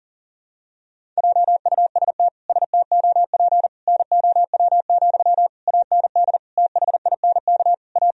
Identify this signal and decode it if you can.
Morse code, heard as a keyed tone, about 40 wpm, 700 Hz. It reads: JUST STOP NOW, AND THINK A
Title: Sentences from Top 1000 Words - Review Part 1 - 40wpm